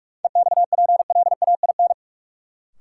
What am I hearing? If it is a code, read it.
Morse code, heard as a keyed tone, about 42 wpm, 700 Hz. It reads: EXPLAIN
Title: Sentences from Top 1100 Words - New Word Drill - 45wpm